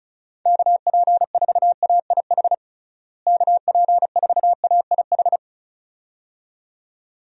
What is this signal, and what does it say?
Morse code, heard as a keyed tone, about 36 wpm, 700 Hz. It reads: KP4AIH KP4AIH
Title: DXCC-100 Call Signs - 6 Characters - Repeated 2x - 35wpm